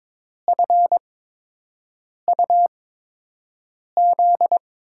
Morse code, heard as a keyed tone, about 22 wpm, spaced as for 6 wpm, 700 Hz. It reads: FUZ
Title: USA Call Signs - 6 Characters - 8x Character Spacing - 22wpm